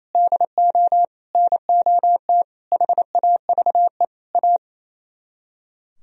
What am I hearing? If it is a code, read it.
Morse code, heard as a keyed tone, about 37 wpm, 700 Hz. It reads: DO NOT HAVE A